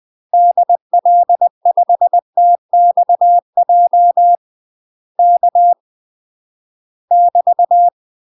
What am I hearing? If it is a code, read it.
Morse code, heard as a keyed tone, about 20 wpm, 700 Hz. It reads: DL5TXJ K <BT>
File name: UK MASTER QSOs TRAINING / 20WPM / UK MASTER QSOs 0020